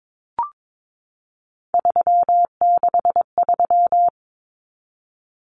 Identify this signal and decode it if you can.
Morse code, heard as a keyed tone, about 22 wpm, 700 Hz. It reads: 363
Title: Common Cut Numbers - 3-Digit 22wpm